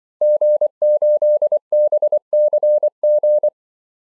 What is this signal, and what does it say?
Morse code, heard as a keyed tone, about 24 wpm, 600 Hz. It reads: G8BCG
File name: zs6nk-qso-cw